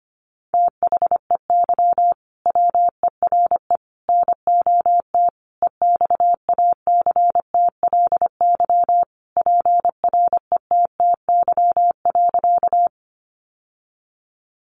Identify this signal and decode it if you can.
Morse code, heard as a keyed tone, about 25 wpm, 700 Hz. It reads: THEY WERE NOT EXACTLY PRETTY.
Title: Sentences from Top 1000 Words 25wpm